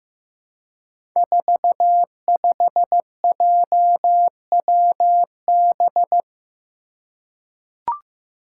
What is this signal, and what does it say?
Morse code, heard as a keyed tone, about 15 wpm, 700 Hz. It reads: 45JWB
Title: Five Challenging Letter-Number Combinations 15wpm